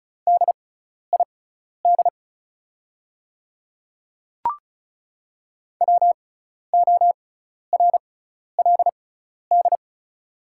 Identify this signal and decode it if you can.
Morse code, heard as a keyed tone, about 36 wpm, spaced as for 12 wpm, 690 Hz. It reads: DID WORLD
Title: 6x Character Spacing - Top 300 Words - 35wpm